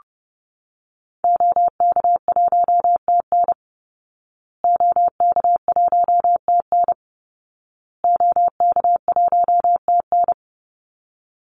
Morse code, heard as a keyed tone, about 30 wpm, 700 Hz. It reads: OK1TN OK1TN OK1TN
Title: Call Signs Repeated 2x - 30wpm